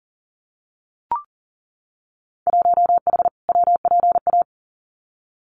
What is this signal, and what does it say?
Morse code, heard as a keyed tone, about 41 wpm, 700 Hz. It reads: 1H WPA
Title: Winter Field Day Exchanges 40wpm